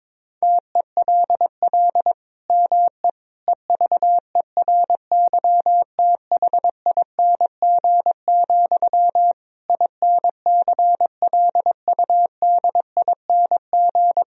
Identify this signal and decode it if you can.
Morse code, heard as a keyed tone, about 21 wpm, 700 Hz. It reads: TELL ME EVERYTHING, INCLUDING
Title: Sentences from Top 1000 Words - Review Part 4 - 22wpm